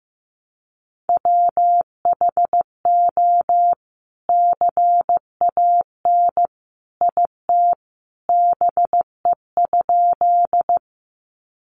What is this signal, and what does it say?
Morse code, heard as a keyed tone, about 15 wpm, 700 Hz. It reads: WHO CAN IT BE?
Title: Sentences from Top 100 Words 15wpm